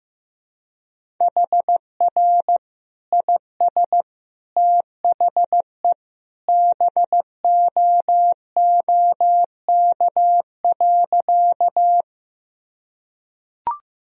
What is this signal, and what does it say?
Morse code, heard as a keyed tone, about 15 wpm, 700 Hz. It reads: HR IS THE BOOK.
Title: Sentences from Top 300 Words and Abbreviations 15wpm